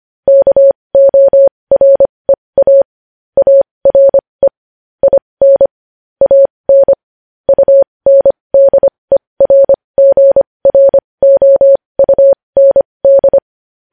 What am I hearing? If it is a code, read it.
Morse code, heard as a keyed tone, about 25 wpm, spaced as for 20 wpm, 550 Hz. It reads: KOREA ARE IN AN UNDERGROUND
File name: podcast / News headlines-1x-25x20